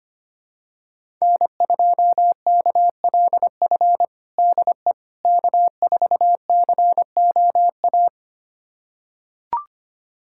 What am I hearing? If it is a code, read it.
Morse code, heard as a keyed tone, about 25 wpm, 700 Hz. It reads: N2KLF DE K4COA
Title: USA Callsign DE Callsign - 5 Characters - 25wpm